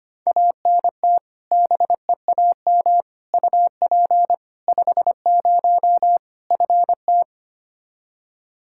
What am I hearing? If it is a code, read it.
Morse code, heard as a keyed tone, about 25 wpm, 700 Hz. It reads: ANT BEAM UP 50 FT
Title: Practice QSO - Exchange 2 - Info - 25wpm